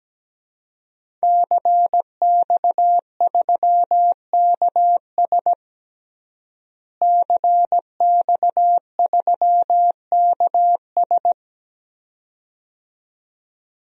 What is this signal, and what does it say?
Morse code, heard as a keyed tone, about 17 wpm, 700 Hz. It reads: CX3KS CX3KS
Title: DXCC-100 Call Signs - 5 Characters - Repeated 2x - 17wpm